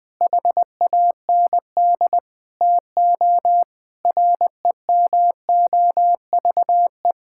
Morse code, heard as a keyed tone, about 20 wpm, 700 Hz. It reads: HAND TO REMOVE
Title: Sentences from Top 1000 Words - Review Part 5 - 20wpm